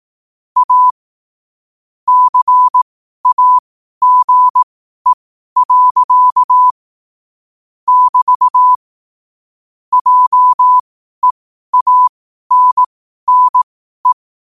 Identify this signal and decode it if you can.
Morse code, heard as a keyed tone, about 18 wpm, spaced as for 12 wpm, 1000 Hz. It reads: A CAGE. <BT> JEANNE
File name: qotdfiles / 12wpm